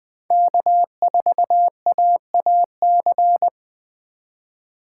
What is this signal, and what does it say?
Morse code, heard as a keyed tone, about 20 wpm, 700 Hz. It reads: K4AAC
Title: K1USN SST QSO Practice - 20wpm